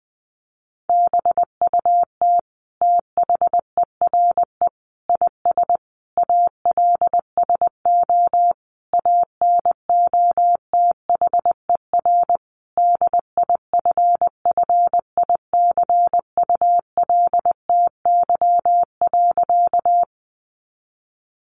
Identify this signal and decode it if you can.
Morse code, heard as a keyed tone, about 20 wpm, 700 Hz. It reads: BUT THERE IS ALSO ANOTHER DIFFICULTY.
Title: Sentences from Top 1500 Words 20wpm